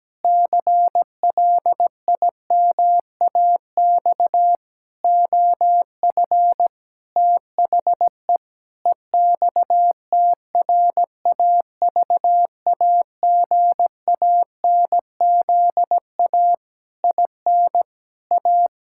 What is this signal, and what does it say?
Morse code, heard as a keyed tone, about 17 wpm, 700 Hz. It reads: CLIMAX OF THE EXTRAVAGANZA IN A
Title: Pangram Challenge v2 - 17wpm